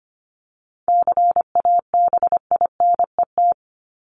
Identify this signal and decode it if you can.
Morse code, heard as a keyed tone, about 25 wpm, 700 Hz. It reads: CABINET
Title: Top 1751-2000 Words 25wpm